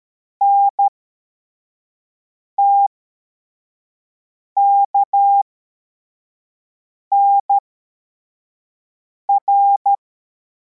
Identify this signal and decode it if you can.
Morse code, heard as a keyed tone, about 13 wpm, spaced as for 4 wpm, 800 Hz. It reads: NTKNR